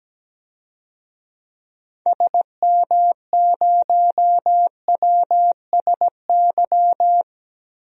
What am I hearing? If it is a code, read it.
Morse code, heard as a keyed tone, about 17 wpm, 700 Hz. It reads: SM0WSY
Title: DXCC-100 Call Signs - 6 Characters - 17wpm